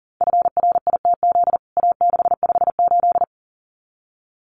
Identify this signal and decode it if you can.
Morse code, heard as a keyed tone, about 40 wpm, 700 Hz. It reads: FRITZ A658
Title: CWT Exchange 40wpm